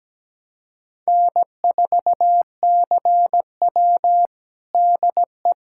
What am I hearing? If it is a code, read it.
Morse code, heard as a keyed tone, about 17 wpm, 700 Hz. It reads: N4CW DE
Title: Practice QSO - Exchange 4 - Ending - 17wpm